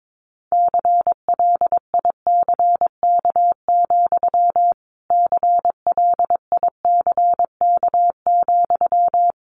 Morse code, heard as a keyed tone, about 22 wpm, 700 Hz. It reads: CLICK, CLICK,
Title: Sentences from Top 1100 Words 22wpm